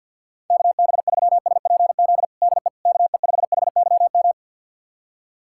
Morse code, heard as a keyed tone, about 51 wpm, 700 Hz. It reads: KB3SPZ DE KE5HYM
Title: USA Callsign DE Callsign - 6 Characters - 50wpm